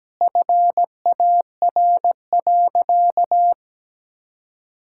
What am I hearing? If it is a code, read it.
Morse code, heard as a keyed tone, about 17 wpm, 700 Hz. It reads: FAR.
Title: Sentences from Top 1000 Words - Review Part 5 - 17wpm